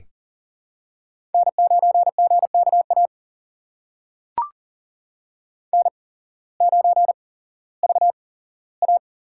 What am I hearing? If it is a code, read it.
Morse code, heard as a keyed tone, about 40 wpm, spaced as for 19 wpm, 700 Hz. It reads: N9GKA N9VA
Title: USA Call Signs - 5 Characters - 8x Character Spacing - 40wpm